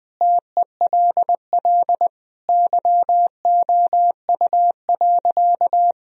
Morse code, heard as a keyed tone, about 21 wpm, 700 Hz. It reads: TELL YOU.
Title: Sentences from Top 1000 Words - Review Part 1 - 20wpm